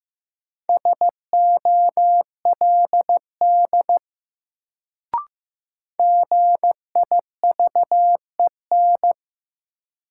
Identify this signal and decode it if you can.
Morse code, heard as a keyed tone, about 15 wpm, 700 Hz. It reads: SOLD GIVEN
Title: Sentences from Top 1400 Words - New Word Drill - 15wpm